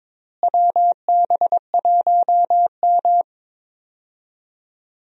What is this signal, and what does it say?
Morse code, heard as a keyed tone, about 22 wpm, 700 Hz. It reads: WB1M
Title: USA Call Signs - 4 Characters - Repeated 2x - 22wpm